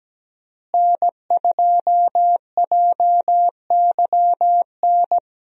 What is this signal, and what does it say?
Morse code, heard as a keyed tone, about 17 wpm, 700 Hz. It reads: N2JYN
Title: USA Call Signs - 5 Characters - Repeated 3x - 17wpm